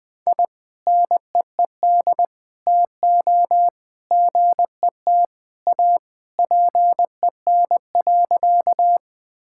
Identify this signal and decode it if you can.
Morse code, heard as a keyed tone, about 20 wpm, 700 Hz. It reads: I NEED TO GET A PEN.
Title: Sentences with 2-Word Phrases - Spoken Beginning - 20wpm